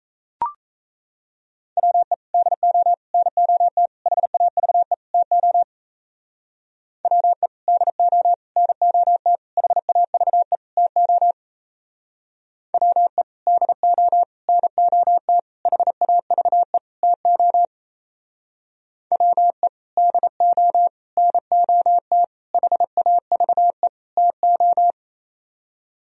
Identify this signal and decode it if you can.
Morse code, heard as a keyed tone, about 37 wpm, 700 Hz. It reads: WE DO NOT HAVE TO WE DO NOT HAVE TO WE DO NOT HAVE TO WE DO NOT HAVE TO